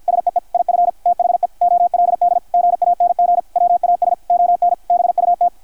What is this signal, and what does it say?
Morse code, heard as a keyed tone, about 50 wpm, 660 Hz, with radio noise. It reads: SEE IF THE OLD GANG WAS ON BUT
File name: archived / CRSnet